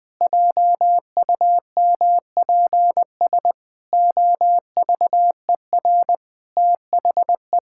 Morse code, heard as a keyed tone, about 19 wpm, 700 Hz. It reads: JUMPS OVER THE
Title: Pangram Challenge v2 - 20wpm